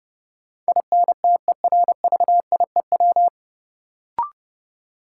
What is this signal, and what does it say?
Morse code, heard as a keyed tone, about 30 wpm, 700 Hz. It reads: INTERVIEW
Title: Sentences from Top 1600 Words - New Word Drill Alt - 30wpm